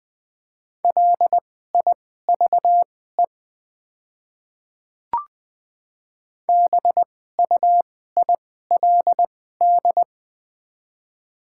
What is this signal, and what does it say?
Morse code, heard as a keyed tone, about 20 wpm, spaced as for 15 wpm, 700 Hz. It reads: LIVE BUILD
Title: 2x Character Spacing - Top 300 Words - 20wpm